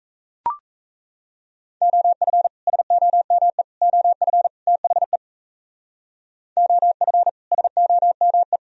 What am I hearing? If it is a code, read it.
Morse code, heard as a keyed tone, about 36 wpm, 700 Hz. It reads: OF SOME OF THE OF SOME